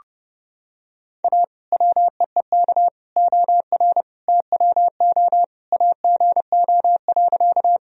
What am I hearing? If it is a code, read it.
Morse code, heard as a keyed tone, about 32 wpm, 700 Hz. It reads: A WEEK OR TWO AGO.
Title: Sentences from Top 1000 Words - Review Part 3 - 30wpm